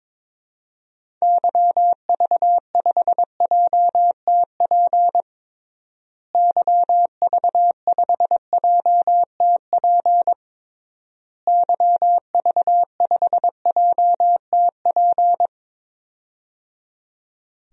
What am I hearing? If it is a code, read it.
Morse code, heard as a keyed tone, about 22 wpm, 700 Hz. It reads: YV5JTP YV5JTP YV5JTP
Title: DXCC-100 Call Signs - 6 Characters - Repeated 3x - 22wpm